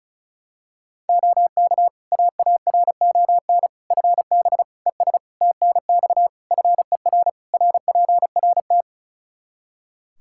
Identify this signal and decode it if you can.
Morse code, heard as a keyed tone, about 36 wpm, 700 Hz. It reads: OK AARON FB ES TNX FER RPRT
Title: K1USN SST QSO Practice 35wpm